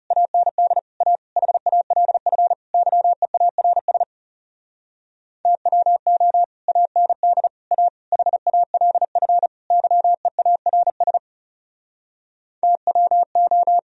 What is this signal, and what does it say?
Morse code, heard as a keyed tone, about 37 wpm, 700 Hz. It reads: AND A HALF YEARS TWO AND A HALF YEARS TWO
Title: Speed-Racing - Sets of 5 Words 30wpm